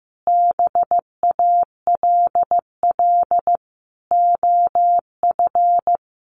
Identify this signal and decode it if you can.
Morse code, heard as a keyed tone, about 15 wpm, 700 Hz. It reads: BALL OF
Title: Easy Phrases 15wpm